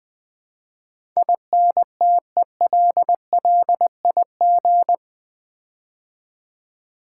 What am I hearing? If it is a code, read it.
Morse code, heard as a keyed tone, about 20 wpm, 700 Hz. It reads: INTELLIG
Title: Build A Word - 4 Letters At A Time - 12-Letter Words - 20wpm